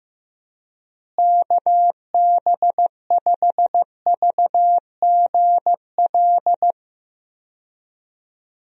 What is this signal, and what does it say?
Morse code, heard as a keyed tone, about 15 wpm, 700 Hz. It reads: KB5VGL
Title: USA Call Signs - 6 Characters - Repeated 3x - 15wpm